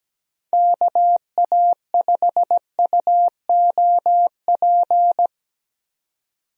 Morse code, heard as a keyed tone, about 17 wpm, 700 Hz. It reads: KA5UOP
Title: USA Call Signs - 6 Characters - 17wpm